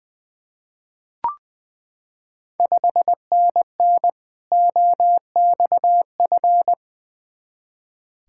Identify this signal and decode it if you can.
Morse code, heard as a keyed tone, about 20 wpm, 700 Hz. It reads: 5NN OXF
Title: Ontario QSO Party Exchange 20wpm